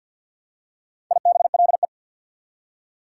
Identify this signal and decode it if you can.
Morse code, heard as a keyed tone, about 54 wpm, 690 Hz. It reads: IBLE
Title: Prefixes and Suffixes 50wpm